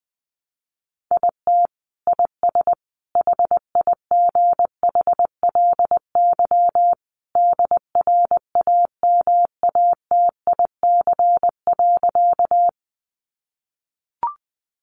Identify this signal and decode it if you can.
Morse code, heard as a keyed tone, about 19 wpm, 700 Hz. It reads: IT IS HIGHLY DRAMATIC.
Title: Sentences from Top 1300 Words 20wpm